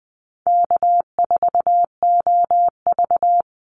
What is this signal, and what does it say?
Morse code, heard as a keyed tone, about 20 wpm, 700 Hz. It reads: K4OV
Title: Easy Call Signs - 20wpm